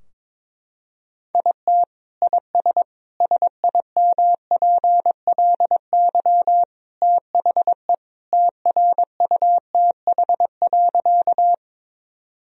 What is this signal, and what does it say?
Morse code, heard as a keyed tone, about 22 wpm, 700 Hz. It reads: IT IS SIMPLY THE TRUTH.
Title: Sentences from Top 1000 Words - Review Part 4 - 22wpm